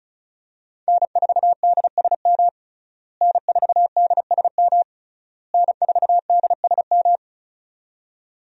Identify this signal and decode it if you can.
Morse code, heard as a keyed tone, about 34 wpm, 700 Hz. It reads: N4DSM N4DSM N4DSM
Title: USA Call Signs - 5 Characters - Repeated 3x - 35wpm